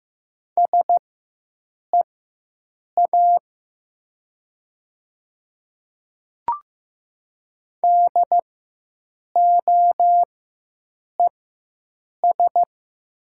Morse code, heard as a keyed tone, about 15 wpm, spaced as for 7 wpm, 700 Hz. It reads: SEA DOES
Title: 4x Character Spacing - Top 300 Words - 15wpm